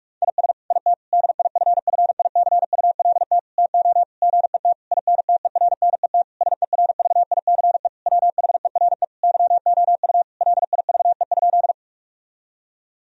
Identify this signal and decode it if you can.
Morse code, heard as a keyed tone, about 46 wpm, 700 Hz. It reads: IS IT DIFFICULT TO GET INTERNET SERVICE WHERE YOU LIVE?
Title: Sentences from Top 1000 Words - Review Part 4 - 45wpm